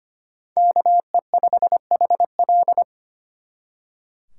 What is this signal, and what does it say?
Morse code, heard as a keyed tone, about 31 wpm, 700 Hz. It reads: KE5HL
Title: Speed Racing - USA 2x2 Call Signs 25wpm